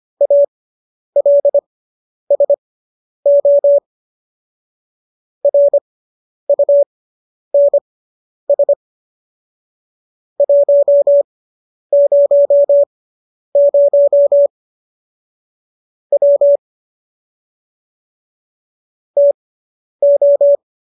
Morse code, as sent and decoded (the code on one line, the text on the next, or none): .- .-.. ... --- / .-. ..- -. ... / .---- ----- ----- / .-- / - ---
ALSO RUNS 100 W TO